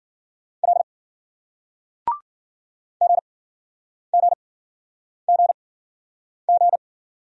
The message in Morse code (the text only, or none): L GGGG